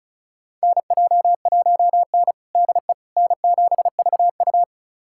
Morse code, heard as a keyed tone, about 34 wpm, 700 Hz. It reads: NJ1N DE N7VU
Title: USA Callsign DE Callsign - 4 Characters - 35wpm